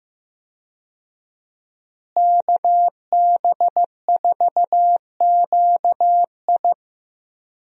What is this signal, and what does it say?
Morse code, heard as a keyed tone, about 15 wpm, 700 Hz. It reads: KB4QI
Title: USA Call Signs - 5 Characters - Repeated 3x - 15wpm